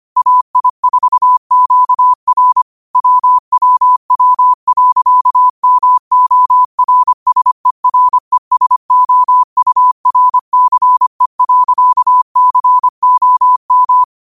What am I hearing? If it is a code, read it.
Morse code, heard as a keyed tone, about 25 wpm, 1000 Hz. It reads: AI4QR WWW.MORSERESOURCE.COM
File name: qotdfiles / 25wpm